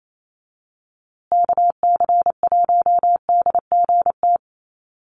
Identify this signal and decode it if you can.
Morse code, heard as a keyed tone, about 28 wpm, 700 Hz. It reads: KC1DGT
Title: USA Call Signs Repeated 3x 28wpm